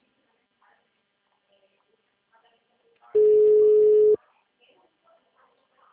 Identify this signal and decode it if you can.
Ringback tone (Europe, 425 Hz), with speech.